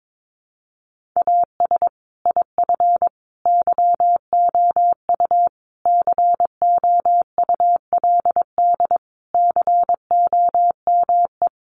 Morse code, heard as a keyed tone, about 22 wpm, 700 Hz. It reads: AS IF YOU COULD COME
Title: Sentences from Top 900 Words 22wpm